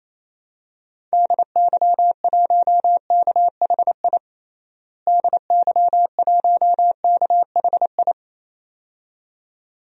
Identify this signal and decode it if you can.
Morse code, heard as a keyed tone, about 28 wpm, 700 Hz. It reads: DY1KHI DY1KHI
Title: DXCC-100 Call Signs - 6 Characters - Repeated 2x - 28wpm